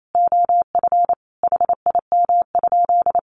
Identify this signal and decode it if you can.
Morse code, heard as a keyed tone, about 28 wpm, 700 Hz. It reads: OF HIM?
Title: Sentences from Top 1500 Words 28wpm